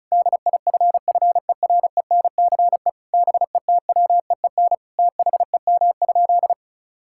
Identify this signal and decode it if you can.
Morse code, heard as a keyed tone, about 36 wpm, 700 Hz. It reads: DIFFERENCE BETWEEN THEM?
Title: Sentences from Top 1000 Words - Review Part 5 - 35wpm